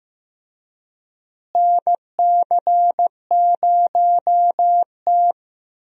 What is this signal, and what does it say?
Morse code, heard as a keyed tone, about 15 wpm, 700 Hz. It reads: NC0T
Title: USA Call Signs - 4 Characters - Repeated 3x - 15wpm